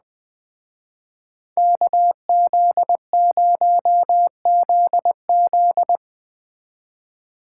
Morse code, heard as a keyed tone, about 20 wpm, 700 Hz. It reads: KZ0ZZ
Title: Speed Racing - USA 2x2 Call Signs 17wpm